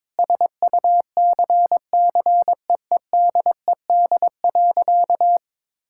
Morse code, heard as a keyed tone, about 22 wpm, 700 Hz. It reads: SUCCEEDED.
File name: Sentences with 7-Word Phrases - 22wpm